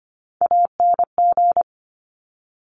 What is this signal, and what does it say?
Morse code, heard as a keyed tone, about 25 wpm, 700 Hz. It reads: ANG
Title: Top Three Letter Combinations in Words 25wpm